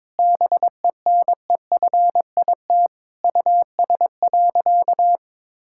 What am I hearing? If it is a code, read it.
Morse code, heard as a keyed tone, about 22 wpm, 700 Hz. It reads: BENEFIT US.
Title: Sentences from Top 1000 Words - Review Part 6 - 22wpm